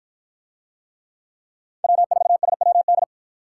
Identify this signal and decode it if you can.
Morse code, heard as a keyed tone, about 53 wpm, 700 Hz. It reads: W4SWD